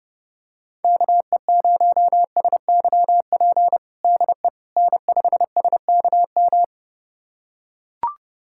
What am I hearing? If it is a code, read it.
Morse code, heard as a keyed tone, about 29 wpm, 700 Hz. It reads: KE0SYP DE N5SKM
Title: USA Callsign DE Callsign 30wpm